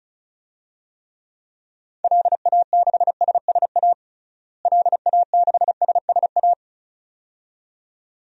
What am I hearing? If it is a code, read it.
Morse code, heard as a keyed tone, about 36 wpm, 700 Hz. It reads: LA6SSA LA6SSA
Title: DXCC-100 Call Signs - 6 Characters - Repeated 2x - 35wpm